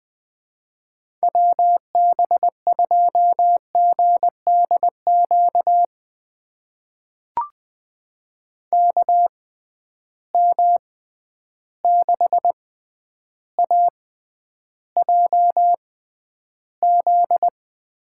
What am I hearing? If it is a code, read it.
Morse code, heard as a keyed tone, about 20 wpm, spaced as for 10 wpm, 700 Hz. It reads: WB2GDQ KM6AJZ
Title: USA Call Signs - 6 Characters - 6x Character Spacing - 20wpm